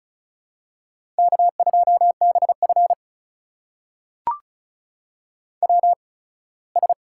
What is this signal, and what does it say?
Morse code, heard as a keyed tone, about 36 wpm, spaced as for 21 wpm, 700 Hz. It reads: K2BF WS
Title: USA Call Signs - 4 Characters - 8x Character Spacing - 35wpm